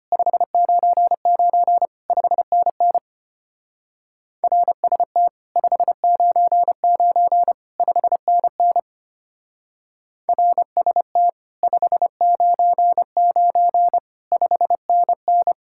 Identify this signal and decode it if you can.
Morse code, heard as a keyed tone, about 31 wpm, 700 Hz. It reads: 599 5NN RST 599 5NN RST 599 5NN
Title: Speed-Racing Common QSO Phrases Encore - 25wpm